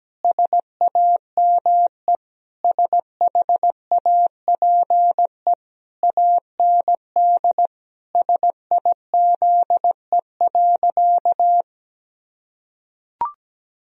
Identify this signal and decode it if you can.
Morse code, heard as a keyed tone, about 17 wpm, 700 Hz. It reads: SAME SHAPE AND SIZE.
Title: Sentences from Top 1000 Words - Review Part 3 - 17wpm